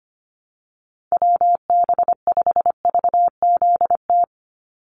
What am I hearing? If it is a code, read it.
Morse code, heard as a keyed tone, about 25 wpm, 700 Hz. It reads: WB5VZT
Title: USA Call Signs 25wpm